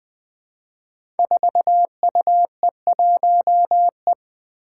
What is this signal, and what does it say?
Morse code, heard as a keyed tone, about 20 wpm, 700 Hz. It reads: 4UE1E
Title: Five Challenging Letter-Number Combinations 20wpm